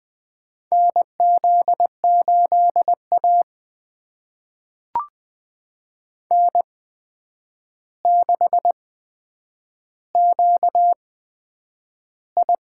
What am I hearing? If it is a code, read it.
Morse code, heard as a keyed tone, about 20 wpm, spaced as for 9 wpm, 700 Hz. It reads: NZ8A N6QI
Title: USA Call Signs - 4 Characters - 8x Character Spacing - 20wpm